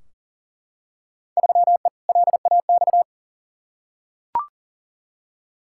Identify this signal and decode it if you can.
Morse code, heard as a keyed tone, about 40 wpm, 700 Hz. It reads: 3E LAX
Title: Field Day Exchanges Encore - 40wpm